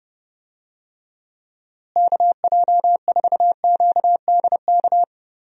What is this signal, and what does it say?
Morse code, heard as a keyed tone, about 32 wpm, 700 Hz. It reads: KJ4QDK
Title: USA Call Signs - 6 Characters - Repeated 3x - 30wpm